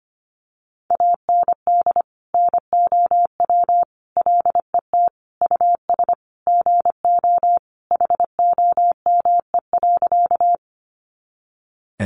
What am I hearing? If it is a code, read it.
Morse code, heard as a keyed tone, about 25 wpm, 700 Hz. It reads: AND NOW LET US GO HOME.
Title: Sentences from Top 300 Words 25wpm